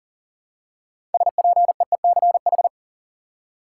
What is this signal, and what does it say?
Morse code, heard as a keyed tone, about 40 wpm, 700 Hz. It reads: SPEECH
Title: Six Letter Words - 40wpm